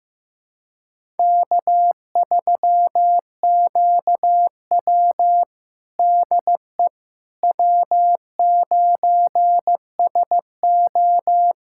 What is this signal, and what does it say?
Morse code, heard as a keyed tone, about 15 wpm, 700 Hz. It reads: K3QW DE W9SO
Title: USA Callsign DE Callsign - 4 Characters - 15wpm